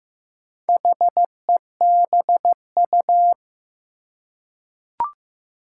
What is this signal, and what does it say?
Morse code, heard as a keyed tone, about 15 wpm, 700 Hz. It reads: HEBU
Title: Four Challenging Letter-Number Combinations 15wpm